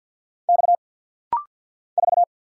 Morse code, heard as a keyed tone, about 50 wpm, 700 Hz. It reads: X 4
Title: Single Letter-Number - Rapid-Fire 50wpm